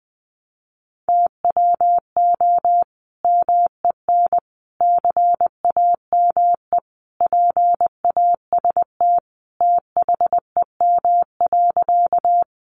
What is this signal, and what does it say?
Morse code, heard as a keyed tone, about 20 wpm, 700 Hz. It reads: TWO MEN CAME PAST THEM.
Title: Sentences from Top 600 Words 20wpm